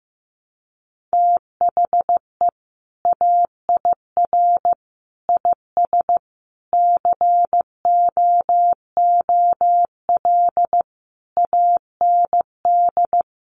Morse code, heard as a keyed tone, about 14 wpm, 700 Hz. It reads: THE AIR IS COOL AND
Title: Sentences from Top 1200 Words 15wpm